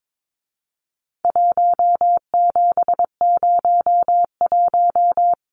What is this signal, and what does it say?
Morse code, heard as a keyed tone, about 22 wpm, 700 Hz. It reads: 1701
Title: Numbers - 4-Digit 22wpm